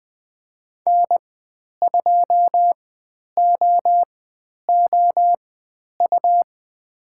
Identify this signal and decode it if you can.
Morse code, heard as a keyed tone, about 20 wpm, spaced as for 10 wpm, 700 Hz. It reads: N2OOU
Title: 109 Lesson 18g - Letter m - Call Signs 20wpm